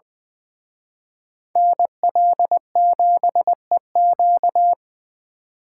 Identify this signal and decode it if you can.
Morse code, heard as a keyed tone, about 23 wpm, 700 Hz. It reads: NL7EQ
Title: Speed Racing - USA 2x2 Call Signs 20wpm